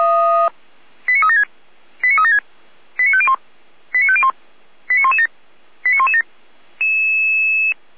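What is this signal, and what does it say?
Five-tone selective call (ZVEI), heard as quick runs of five tones, tones of about 70 ms. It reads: 89376 89376 89501 89501 89117 89117
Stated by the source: ZVEI five-tone selective call received by SDR.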